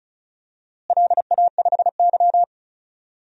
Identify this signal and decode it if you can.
Morse code, heard as a keyed tone, about 36 wpm, 700 Hz. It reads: LA5Y
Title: DXCC-100 Call Signs - 4 Characters - Repeated 2x - 35wpm